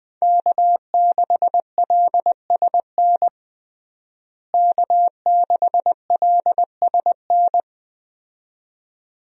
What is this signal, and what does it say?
Morse code, heard as a keyed tone, about 20 wpm, 700 Hz. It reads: K6LSN K6LSN
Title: USA Call Signs - 5 Characters - Repeated 2x - 20wpm